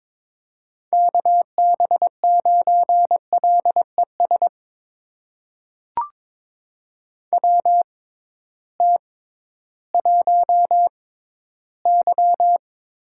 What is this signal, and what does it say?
Morse code, heard as a keyed tone, about 22 wpm, spaced as for 13 wpm, 700 Hz. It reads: KB9LES WT1Y
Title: USA Call Signs - 6x Character Spacing - 22wpm